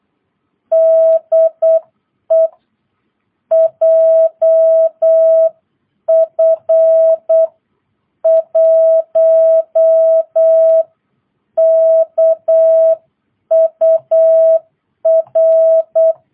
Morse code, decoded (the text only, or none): DE JF1KUR